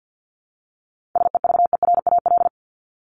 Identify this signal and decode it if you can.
Morse code, heard as a keyed tone, about 50 wpm, 700 Hz. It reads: SEVERAL
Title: Top 500 Words Repeated 50wpm